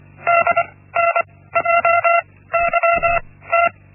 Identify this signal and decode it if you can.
Morse code, heard as a keyed tone, about 24 wpm, spaced as for 16 wpm, 680 Hz. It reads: DNJYT
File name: Linksmakalnis-1